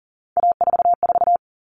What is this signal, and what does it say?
Morse code, heard as a keyed tone, about 40 wpm, 700 Hz. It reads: A44
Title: Common Cut Numbers - 3-Digit 40wpm